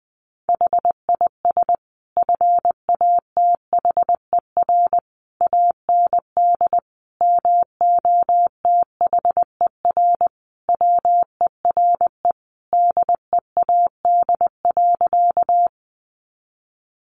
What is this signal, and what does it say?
Morse code, heard as a keyed tone, about 20 wpm, 700 Hz. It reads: HIS FATHER AND MOTHER WERE DEAD.
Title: Sentences from Top 1500 Words 20wpm